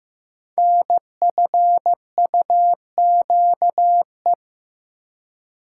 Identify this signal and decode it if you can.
Morse code, heard as a keyed tone, about 15 wpm, 700 Hz. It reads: NFUQE
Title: Code Groups - 5 Letter:Numbers - 15wpm